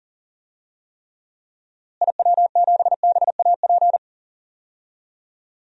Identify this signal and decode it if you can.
Morse code, heard as a keyed tone, about 40 wpm, 700 Hz. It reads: IW7BAP
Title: DXCC-100 Call Signs - 6 Characters - Repeated 3x - 40wpm